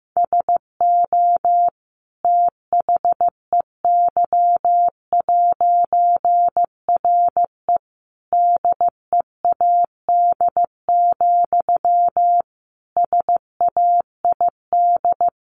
Morse code, heard as a keyed tone, about 15 wpm, 700 Hz. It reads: SO THEY'RE DEAD, SAID
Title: A Sound Of Thunder 15wpm